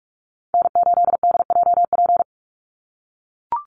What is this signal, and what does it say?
Morse code, heard as a keyed tone, about 44 wpm, 700 Hz. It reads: N8DJP
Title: USA Call Signs 45wpm